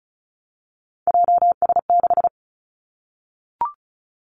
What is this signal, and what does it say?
Morse code, heard as a keyed tone, about 36 wpm, 700 Hz. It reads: JS6
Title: Three Challenging Letter-Number Combinations 35wpm